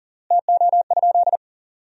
Morse code, heard as a keyed tone, about 40 wpm, 700 Hz. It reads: TO?
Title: 4x Word-Spacing - Sentences from Top 100 Words - 40wpm